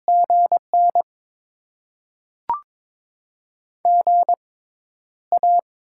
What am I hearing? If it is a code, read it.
Morse code, heard as a keyed tone, about 22 wpm, spaced as for 11 wpm, 700 Hz. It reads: GN GA
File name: Top 20 QSO Elements - 6x Character Spacing - 22wpm